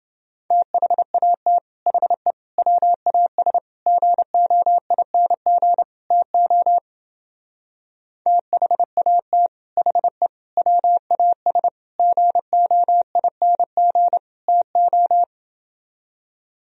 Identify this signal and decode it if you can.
Morse code, heard as a keyed tone, about 28 wpm, 700 Hz. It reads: THAT HE WAS GOING TO THAT HE WAS GOING TO